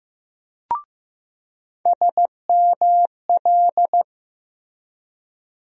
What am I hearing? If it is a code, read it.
Morse code, heard as a keyed tone, about 15 wpm, 700 Hz. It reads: SML
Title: USA Call Sign Postfixes - 3 Letters 15wpm